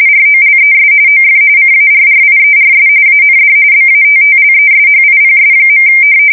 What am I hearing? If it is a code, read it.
RTTY, about 45.45 baud. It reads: ROLES OF CARRIERS AND BATTLESHIPS.
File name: fsk170 / battleshipnj